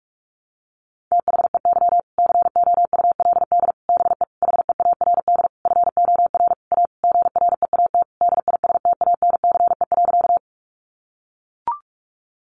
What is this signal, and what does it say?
Morse code, heard as a keyed tone, about 44 wpm, 700 Hz. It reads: THEY COULD BE HEARD FOR A GREAT DISTANCE.
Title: Sentences from Top 1200 Words 45wpm